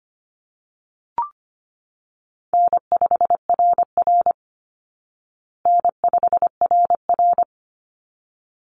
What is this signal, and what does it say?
Morse code, heard as a keyed tone, about 25 wpm, 700 Hz. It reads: N5RR N5RR
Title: Easy Call Signs Repeated 2x - 25wpm